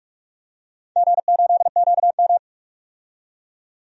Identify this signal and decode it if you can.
Morse code, heard as a keyed tone, about 43 wpm, 700 Hz. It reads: G8QM
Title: DXCC-100 Call Signs - 4 Characters - 45wpm